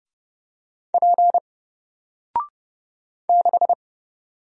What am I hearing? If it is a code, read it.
Morse code, heard as a keyed tone, about 30 wpm, 700 Hz. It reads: P 6
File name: Single-Letter-Number-30wpm